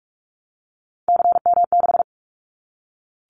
Morse code, heard as a keyed tone, about 44 wpm, 700 Hz. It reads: CM6
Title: Callsign Prefixes for Top 100 DXCC - 45wpm